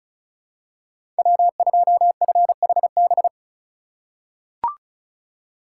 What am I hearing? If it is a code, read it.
Morse code, heard as a keyed tone, about 36 wpm, 700 Hz. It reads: W2FHB
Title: USA Call Signs - 3x Character Spacing - 35wpm